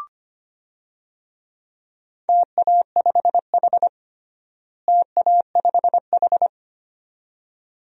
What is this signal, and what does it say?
Morse code, heard as a keyed tone, about 25 wpm, 700 Hz. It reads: TA5H TA5H
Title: DXCC-100 Call Signs - 4 Characters - Repeated 2x - 25wpm